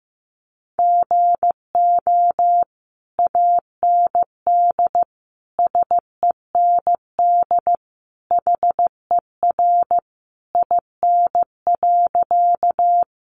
Morse code, heard as a keyed tone, about 15 wpm, 700 Hz. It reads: GO AND SEND HER IN.
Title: Sentences from Top 1500 Words 15wpm